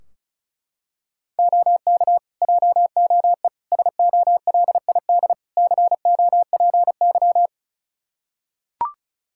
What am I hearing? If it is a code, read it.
Morse code, heard as a keyed tone, about 36 wpm, 700 Hz. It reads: OK JOE SOLID COPY
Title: K1USN SST QSO Practice 35wpm